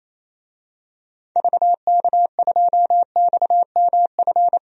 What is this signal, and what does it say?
Morse code, heard as a keyed tone, about 28 wpm, 700 Hz. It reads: VK2XMF
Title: DXCC-100 Call Signs - 6 Characters - 28wpm